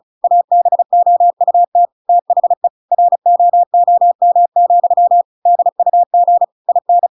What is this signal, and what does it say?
Morse code, heard as a keyed tone, about 34 wpm, 700 Hz. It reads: ABOUT THE ROOM, DUG IN
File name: Pinocchio / Pinocchio ch5 35 35 ik1hge